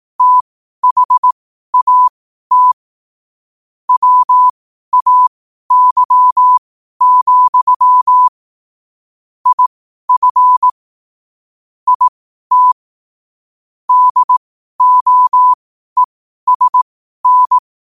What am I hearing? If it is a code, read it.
Morse code, heard as a keyed tone, about 18 wpm, spaced as for 12 wpm, 1000 Hz. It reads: THAT WAY, IF IT DOESN
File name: qotdfiles / 12wpm